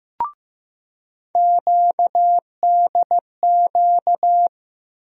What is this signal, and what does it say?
Morse code, heard as a keyed tone, about 15 wpm, 700 Hz. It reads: QDQ
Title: USA Call Sign Postfixes - 3 Letters 15wpm